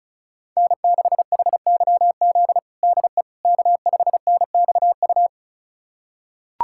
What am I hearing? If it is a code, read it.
Morse code, heard as a keyed tone, about 36 wpm, 700 Hz. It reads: N6HYZ DE K5NXU
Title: USA Callsign DE Callsign - 5 Characters - 35wpm